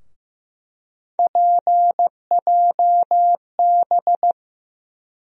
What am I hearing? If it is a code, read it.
Morse code, heard as a keyed tone, about 15 wpm, 700 Hz. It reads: PJB
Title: USA Call Sign Postfixes - 3 Letters 15wpm